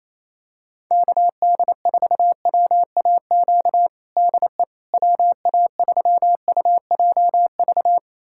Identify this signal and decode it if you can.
Morse code, heard as a keyed tone, about 28 wpm, 700 Hz. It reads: KD4WAQ DE WA3UJV
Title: USA Callsign DE Callsign 28wpm